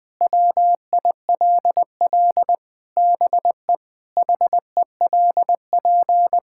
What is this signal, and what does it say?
Morse code, heard as a keyed tone, about 20 wpm, 700 Hz. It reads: WILL BE HELP
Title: Sentences from Top 1000 Words - Review Part 5 - 20wpm